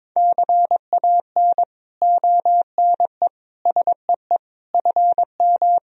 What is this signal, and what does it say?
Morse code, heard as a keyed tone, about 22 wpm, 700 Hz. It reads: CAN ONE SEE FM
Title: Sentences from Top 100 Words and Abbreviations 22wpm